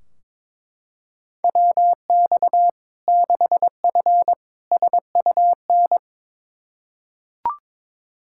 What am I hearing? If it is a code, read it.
Morse code, heard as a keyed tone, about 21 wpm, 700 Hz. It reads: WX 6F SUN
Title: Practice QSO - Exchange 2 - Info - 22wpm